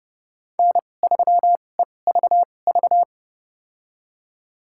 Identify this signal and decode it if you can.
Morse code, heard as a keyed tone, about 32 wpm, spaced as for 22 wpm, 700 Hz. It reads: N3EVV
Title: USA Call Signs - 5 Characters - 2x Character Spacing - 30wpm